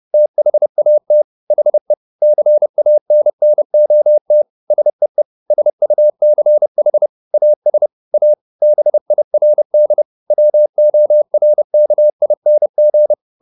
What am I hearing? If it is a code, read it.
Morse code, heard as a keyed tone, about 30 wpm, 600 Hz. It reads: THAT HE CANNOT SEE SUCH AS A BIRD WORKING
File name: the old man and the sea / the old man and the sea 1 30wpm